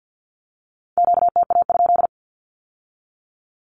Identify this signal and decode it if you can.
Morse code, heard as a keyed tone, about 50 wpm, 700 Hz. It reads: QTA?
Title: Q-Codes with Spoken Meaning - 50wpm